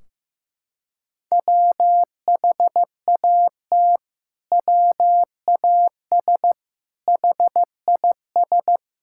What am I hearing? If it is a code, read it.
Morse code, heard as a keyed tone, about 15 wpm, 700 Hz. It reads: WHAT WAS HIS
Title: Sentences from Top 1600 Words 15wpm